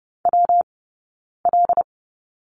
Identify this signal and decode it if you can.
Morse code, heard as a keyed tone, about 30 wpm, 700 Hz. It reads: W L
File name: Lesson 19d - Letter w - Two Characters 30wpm